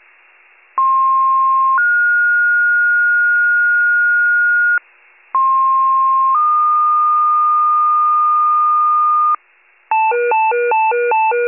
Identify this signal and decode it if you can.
Two-tone page, heard as a one-second tone, then a three-second tone, about 1060 then 1430 Hz, about 1060 then 1220 Hz, with noise. It is heard from 0.8 to 9.4 s.